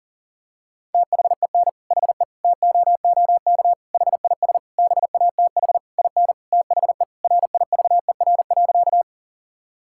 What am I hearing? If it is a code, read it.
Morse code, heard as a keyed tone, about 37 wpm, 700 Hz. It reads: THEN HE TOOK HIS BATH IN THE RIVER.
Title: Sentences from Top 1600 Words 40wpm